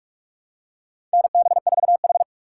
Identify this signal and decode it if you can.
Morse code, heard as a keyed tone, about 46 wpm, 690 Hz. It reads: NB4H
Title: USA Call Signs - 4 Characters - 3x Character Spacing - 45wpm